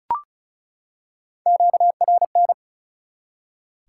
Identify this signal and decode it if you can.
Morse code, heard as a keyed tone, about 34 wpm, 700 Hz. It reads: QRN
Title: Top 180 QSO Elements 35wpm